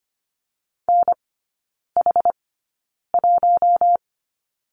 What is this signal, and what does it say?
Morse code, heard as a keyed tone, about 25 wpm, spaced as for 9 wpm, 700 Hz. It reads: NH1
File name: course / 25:10wpm / 093 Lesson 16e - Letter c - Three Characters 25wpm